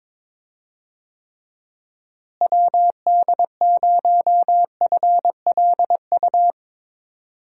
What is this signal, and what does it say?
Morse code, heard as a keyed tone, about 22 wpm, 700 Hz. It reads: WD0FLU
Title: USA Call Signs - 6 Characters - Repeated 3x - 22wpm